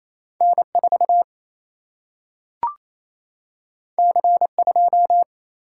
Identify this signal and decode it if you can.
Morse code, heard as a keyed tone, about 28 wpm, 700 Hz. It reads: N 4 C 2
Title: Single-Letter Single-Number 28wpm